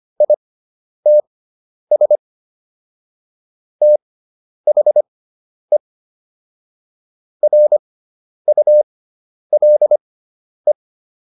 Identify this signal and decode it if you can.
Morse code, heard as a keyed tone, about 25 wpm, spaced as for 10 wpm, 600 Hz. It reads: ITS THE RULE